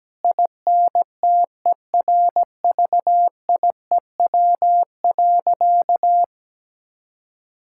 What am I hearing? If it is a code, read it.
Morse code, heard as a keyed tone, about 17 wpm, 700 Hz. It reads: INTERVIEW.
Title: Sentences from Top 1600 Words 17wpm